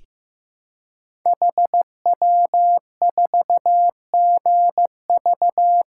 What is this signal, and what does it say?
Morse code, heard as a keyed tone, about 15 wpm, 700 Hz. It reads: HW4GV
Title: Code Groups - 5 Letter:Numbers - 15wpm